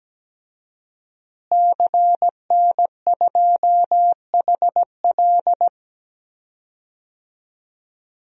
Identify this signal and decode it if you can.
Morse code, heard as a keyed tone, about 17 wpm, 700 Hz. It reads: CN2HL
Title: DXCC-100 Call Signs - 5 Characters - Repeated 3x - 17wpm